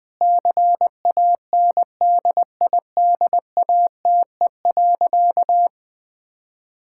Morse code, heard as a keyed tone, about 20 wpm, 700 Hz. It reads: CANDIDATE.
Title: Sentences from Top 1600 Words 20wpm